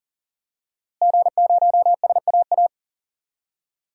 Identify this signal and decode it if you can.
Morse code, heard as a keyed tone, about 40 wpm, 700 Hz. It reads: G0SAA
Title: DXCC-100 Call Signs - 5 Characters - Repeated 2x - 40wpm